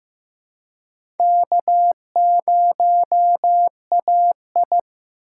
Morse code, heard as a keyed tone, about 15 wpm, 700 Hz. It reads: K0AI
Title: USA Call Signs - 4 Characters - 3x Character Spacing - 15wpm